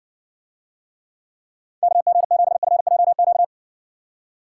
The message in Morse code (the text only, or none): KG7FJQ